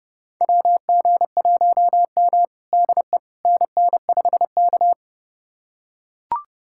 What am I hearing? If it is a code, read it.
Morse code, heard as a keyed tone, about 32 wpm, 700 Hz. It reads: WG1M DE NN5K
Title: USA Callsign DE Callsign - 4 Characters - 30wpm